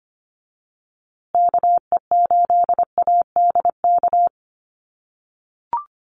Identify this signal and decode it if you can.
Morse code, heard as a keyed tone, about 25 wpm, 700 Hz. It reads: KE8ADK
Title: USA Call Signs Repeated 2x 25wpm